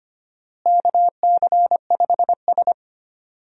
Morse code, heard as a keyed tone, about 25 wpm, 700 Hz. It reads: KC5S
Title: USA Call Signs - 4 Characters - Repeated 2x - 25wpm